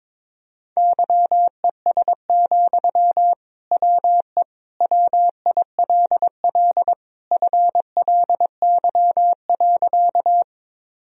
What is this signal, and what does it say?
Morse code, heard as a keyed tone, about 21 wpm, 700 Hz. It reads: YES, WE WILL FLY.
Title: Sentences from Top 1000 Words - Review Part 2 - 22wpm